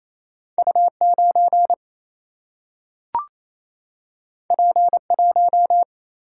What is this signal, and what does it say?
Morse code, heard as a keyed tone, about 28 wpm, 700 Hz. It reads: U 9 P 1
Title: Single-Letter Single-Number 28wpm